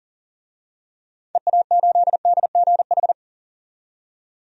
Morse code, heard as a keyed tone, about 37 wpm, 700 Hz. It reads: EA8DGH
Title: DXCC-100 Call Signs - 6 Characters - 40wpm